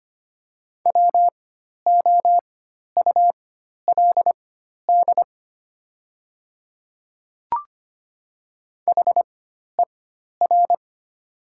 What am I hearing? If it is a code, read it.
Morse code, heard as a keyed tone, about 25 wpm, spaced as for 12 wpm, 690 Hz. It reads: WOULD HER
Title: 4x Character Spacing - Top 200 Words - 25wpm